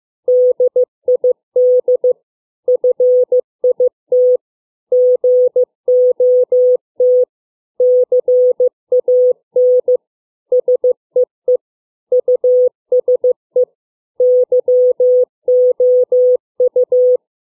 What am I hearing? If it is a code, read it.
Morse code, heard as a keyed tone, about 15 wpm, 500 Hz. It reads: DID FIT GOT CAN SEE USE YOU